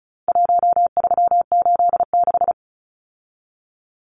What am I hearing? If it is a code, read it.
Morse code, heard as a keyed tone, about 34 wpm, 700 Hz. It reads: 1386
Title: Numbers - Contest Style - 4-Digit 35wpm